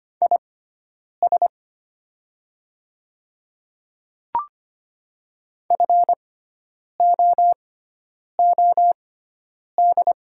Morse code, heard as a keyed tone, about 25 wpm, spaced as for 9 wpm, 700 Hz. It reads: IS FOOD
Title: 6x Character Spacing - Top 500 Words - 25wpm